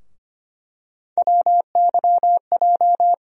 Morse code, heard as a keyed tone, about 25 wpm, 700 Hz. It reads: WYJ
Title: USA Call Sign Postfixes - 3 Letters 25wpm